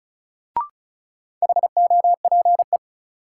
Morse code, heard as a keyed tone, about 36 wpm, 700 Hz. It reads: HOPE
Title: Sentences from Top 1000 Words - New Word Drill - 35wpm